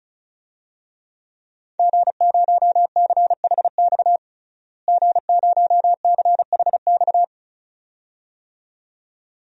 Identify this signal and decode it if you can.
Morse code, heard as a keyed tone, about 34 wpm, 700 Hz. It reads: G0CHX G0CHX
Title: DXCC-100 Call Signs - 5 Characters - Repeated 2x - 35wpm